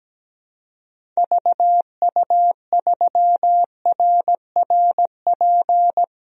Morse code, heard as a keyed tone, about 17 wpm, 700 Hz. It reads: VU3RRP
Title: DXCC-100 Call Signs - 6 Characters - Repeated 2x - 17wpm